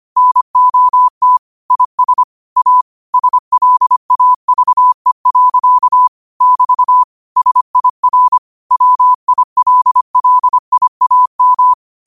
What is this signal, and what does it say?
Morse code, heard as a keyed tone, about 25 wpm, 1000 Hz. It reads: NOT IS A SLAVE. <BT> SIR WILLIAM